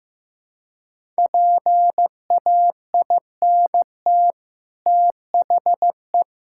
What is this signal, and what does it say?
Morse code, heard as a keyed tone, about 14 wpm, 700 Hz. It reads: PAINT THE
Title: Sentences from Top 1000 Words - Review Part 6 - 15wpm